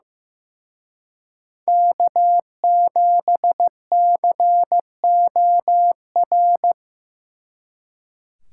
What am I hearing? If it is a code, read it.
Morse code, heard as a keyed tone, about 18 wpm, 700 Hz. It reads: K7COR
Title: Speed Racing - USA Call Signs 15wpm